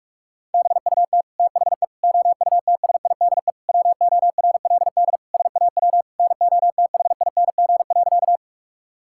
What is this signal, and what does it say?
Morse code, heard as a keyed tone, about 41 wpm, 700 Hz. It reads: BUT THE OUTSIDE WORLD SAW NOTHING.
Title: Sentences from Top 1000 Words - Review Part 6 - 45wpm